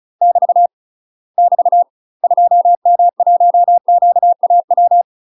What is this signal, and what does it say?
Morse code, heard as a keyed tone, about 36 wpm, 700 Hz. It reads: <BT> <BT> 2M1QAW
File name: UK MASTER QSOs TRAINING / 35WPM / UK MASTER QSOs 0028